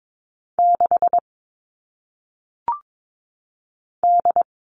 Morse code, heard as a keyed tone, about 22 wpm, 700 Hz. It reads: 6 D
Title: All Cut Numbers - 1-Digit 22wpm